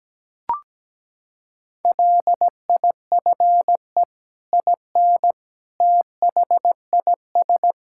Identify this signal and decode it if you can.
Morse code, heard as a keyed tone, about 17 wpm, 700 Hz. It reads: LIFE IN THIS
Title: Sentences from Top 1000 Words - Review Part 5 - 17wpm